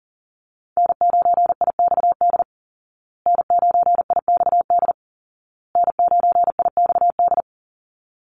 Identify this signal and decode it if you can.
Morse code, heard as a keyed tone, about 40 wpm, 700 Hz. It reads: N9IXD N9IXD N9IXD
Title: Call Signs Repeated 2x - 40wpm